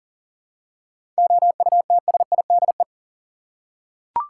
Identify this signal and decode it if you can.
Morse code, heard as a keyed tone, about 37 wpm, 700 Hz. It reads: OUTSIDE
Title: Seven Letter Words - 40wpm